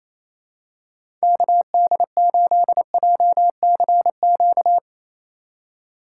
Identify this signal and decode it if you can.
Morse code, heard as a keyed tone, about 28 wpm, 700 Hz. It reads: KD8JCQ
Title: USA Call Signs - 6 Characters - 4x Character Spacing - 28wpm